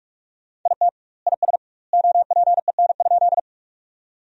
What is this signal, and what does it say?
Morse code, heard as a keyed tone, about 46 wpm, 700 Hz. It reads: IT IS OPEN?
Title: 2x Word-Spacing - Sentences from Top 300 Words - 45wpm